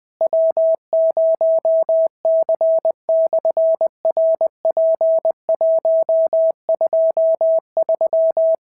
Morse code, heard as a keyed tone, about 20 wpm, 650 Hz. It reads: W0C/RP123
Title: SOTA QSO Practice 20wpm